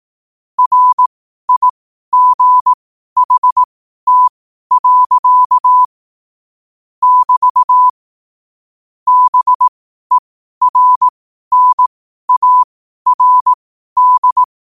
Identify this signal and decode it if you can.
Morse code, heard as a keyed tone, about 18 wpm, spaced as for 12 wpm, 1000 Hz. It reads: RIGHT. <BT> BERNARD
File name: qotdfiles / 12wpm